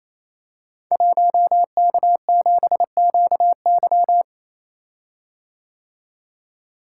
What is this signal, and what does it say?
Morse code, heard as a keyed tone, about 27 wpm, 700 Hz. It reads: 1K7QY
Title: Code Groups - 5 Letter:Numbers - 28wpm